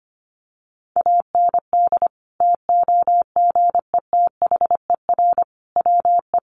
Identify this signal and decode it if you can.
Morse code, heard as a keyed tone, about 25 wpm, 700 Hz. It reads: AND TOGETHER WE
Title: Sentences from Top 600 Words 25wpm